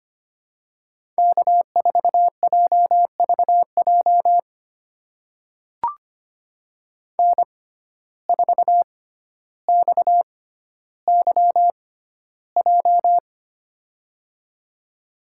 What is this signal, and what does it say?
Morse code, heard as a keyed tone, about 25 wpm, spaced as for 13 wpm, 700 Hz. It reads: K4JVJ N4XYJ
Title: USA Call Signs - 6x Character Spacing - 25wpm